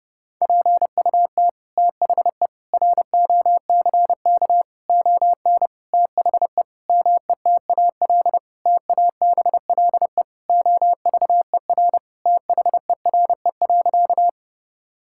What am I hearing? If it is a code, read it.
Morse code, heard as a keyed tone, about 30 wpm, 700 Hz. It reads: PUT THE ROCK ON THE METAL TABLE OVER THERE.
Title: Sentences from Top 1000 Words - Review Part 6 - 30wpm